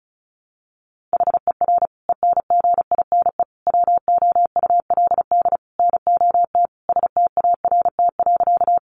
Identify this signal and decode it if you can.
Morse code, heard as a keyed tone, about 34 wpm, 700 Hz. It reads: HER ENGINE WOULD NOT START.
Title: Sentences from Top 1500 Words 35wpm